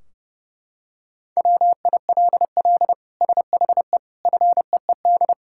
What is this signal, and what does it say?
Morse code, heard as a keyed tone, about 30 wpm, 700 Hz. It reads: WILL SHE FEED
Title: Sentences from Top 1600 Words 30wpm